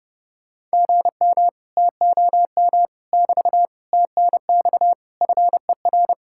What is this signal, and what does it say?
Morse code, heard as a keyed tone, about 33 wpm, 700 Hz. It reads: GM TOM <BT> TNX FER